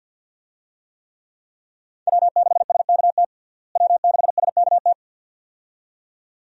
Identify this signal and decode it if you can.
Morse code, heard as a keyed tone, about 52 wpm, 700 Hz. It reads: W6SKT W6SKT
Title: USA Call Signs - 5 Characters - Repeated 2x - 50wpm